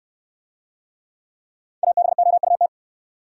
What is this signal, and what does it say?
Morse code, heard as a keyed tone, about 69 wpm, 700 Hz. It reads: ABOUT